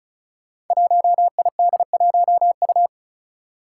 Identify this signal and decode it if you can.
Morse code, heard as a keyed tone, about 36 wpm, 700 Hz. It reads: 1ID1U
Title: Five Challenging Letter-Number Combinations 35wpm